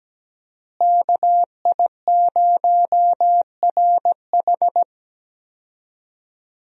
Morse code, heard as a keyed tone, about 17 wpm, 700 Hz. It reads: KI0RH
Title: USA Call Signs - 5 Characters - Repeated 2x - 17wpm